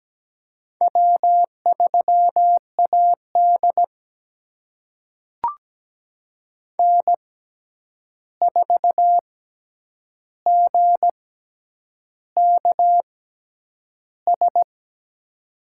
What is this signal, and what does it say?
Morse code, heard as a keyed tone, about 17 wpm, spaced as for 8 wpm, 700 Hz. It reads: W3AD N4GKS
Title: USA Call Signs - 6x Character Spacing - 17wpm